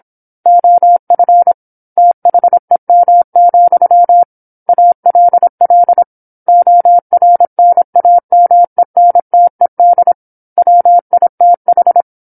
Morse code, heard as a keyed tone, about 27 wpm, 700 Hz. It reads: OF THEM, ALL ORNAMENTED WITH
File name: 08-Chapter-08 26 WPM